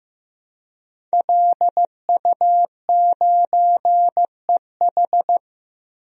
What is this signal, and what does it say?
Morse code, heard as a keyed tone, about 15 wpm, 700 Hz. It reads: LU9EH
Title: DXCC-100 Call Signs - 5 Characters - Repeated 3x - 15wpm